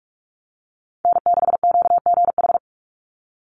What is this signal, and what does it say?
Morse code, heard as a keyed tone, about 44 wpm, 700 Hz. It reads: N6QGH
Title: USA Call Signs 45wpm